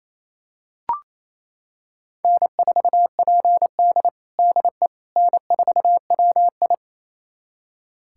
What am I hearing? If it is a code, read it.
Morse code, heard as a keyed tone, about 28 wpm, 700 Hz. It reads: N4PD DE N4WI
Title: USA Callsign DE Callsign - 4 Characters - 28wpm